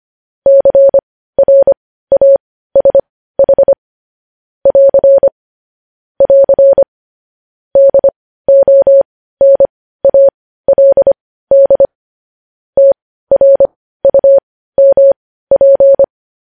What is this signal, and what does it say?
Morse code, heard as a keyed tone, about 25 wpm, spaced as for 15 wpm, 550 Hz. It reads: CRASH <AR> <AR> DONALD TRUMP
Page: News Headlines in Morse Code at 15 WPM